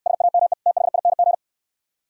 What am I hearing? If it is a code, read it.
Morse code, heard as a keyed tone, about 69 wpm, 700 Hz. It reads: SAME THING